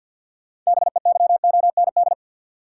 Morse code, heard as a keyed tone, about 47 wpm, 700 Hz. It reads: BEYOND
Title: Sentences from Top 1000 Words - Review Part 6 - New Word Drill - 50wpm